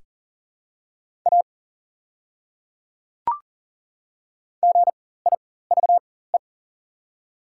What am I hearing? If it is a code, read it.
Morse code, heard as a keyed tone, about 37 wpm, spaced as for 18 wpm, 700 Hz. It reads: A GIVE
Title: 4x Character Spacing - Top 200 Words - 40wpm